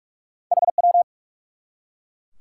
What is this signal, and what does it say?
Morse code, heard as a keyed tone, about 43 wpm, 700 Hz. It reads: HW
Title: Top 90 QSO Elements 45wpm